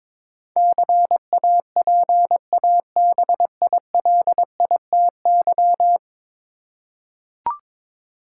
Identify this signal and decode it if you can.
Morse code, heard as a keyed tone, about 22 wpm, 700 Hz. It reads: CAPABILITY
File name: practice / 22wpm / 10 Letter Words - First 3 Letters Given - 22wpm